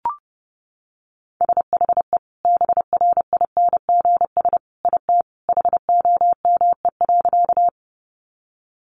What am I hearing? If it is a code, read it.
Morse code, heard as a keyed tone, about 30 wpm, 700 Hz. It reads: SHE BRINGS IT HOME.
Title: Easy Phrases 30wpm